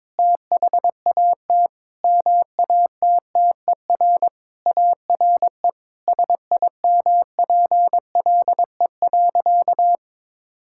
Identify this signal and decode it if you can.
Morse code, heard as a keyed tone, about 22 wpm, 700 Hz. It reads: THAT MATTER ARE SIMPLE.
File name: practice / 22wpm / Sentences with 3-Word Phrases - 22wpm